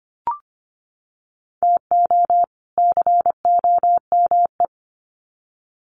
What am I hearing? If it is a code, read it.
Morse code, heard as a keyed tone, about 25 wpm, 700 Hz. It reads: TO COME
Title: Sets of 2 Words 25wpm